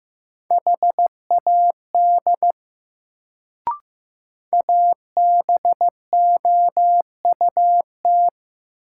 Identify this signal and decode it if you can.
Morse code, heard as a keyed tone, about 15 wpm, 700 Hz. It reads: HAD ABOUT
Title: Sentences from Top 100 Words - New Word Drill - 15wpm